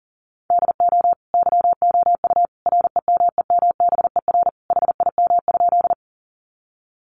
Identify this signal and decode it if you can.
Morse code, heard as a keyed tone, about 40 wpm, 700 Hz. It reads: DO YOU REMEMBER HIM?
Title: Sentences from Top 400 Words 40wpm